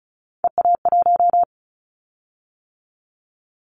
Morse code, heard as a keyed tone, about 36 wpm, 700 Hz. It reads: EA1
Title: Callsign Prefixes for Top 25 DXCC - 35wpm